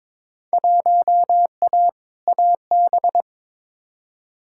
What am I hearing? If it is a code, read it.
Morse code, heard as a keyed tone, about 23 wpm, 700 Hz. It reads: 1A AB
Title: Field Day Exchanges - 22wpm